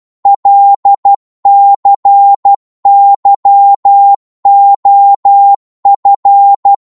Morse code, heard as a keyed tone, about 12 wpm, 800 Hz. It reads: LCYOF
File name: Morse / ltr-12wpm